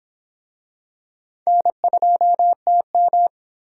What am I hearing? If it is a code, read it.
Morse code, heard as a keyed tone, about 26 wpm, 700 Hz. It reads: N2TM